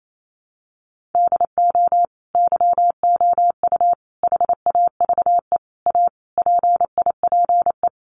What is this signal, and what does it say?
Morse code, heard as a keyed tone, about 28 wpm, 700 Hz. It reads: DO YOU HAVE A PIPE
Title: Sentences from Top 1500 Words 28wpm